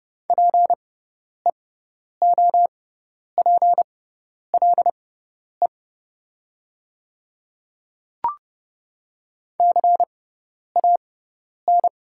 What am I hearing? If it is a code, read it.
Morse code, heard as a keyed tone, about 30 wpm, spaced as for 10 wpm, 700 Hz. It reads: PEOPLE CAN
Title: 6x Character Spacing - Top 100 Words - 30wpm